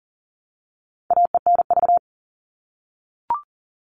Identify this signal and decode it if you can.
Morse code, heard as a keyed tone, about 37 wpm, 700 Hz. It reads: AENV
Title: All Cut Numbers - 4-Digit 40wpm